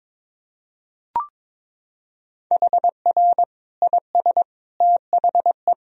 Morse code, heard as a keyed tone, about 22 wpm, 700 Hz. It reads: HR IS THE
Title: Sentences from Top 300 Words and Abbreviations 22wpm